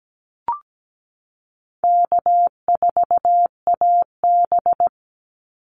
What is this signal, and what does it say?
Morse code, heard as a keyed tone, about 17 wpm, 700 Hz. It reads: K4AB
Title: Easy Call Signs Repeated 3x - 17wpm